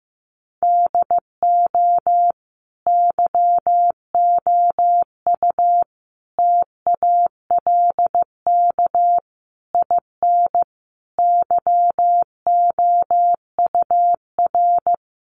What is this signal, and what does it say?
Morse code, heard as a keyed tone, about 14 wpm, 700 Hz. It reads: DO YOU TALK IN YOUR
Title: Sentences from Top 500 Words 15wpm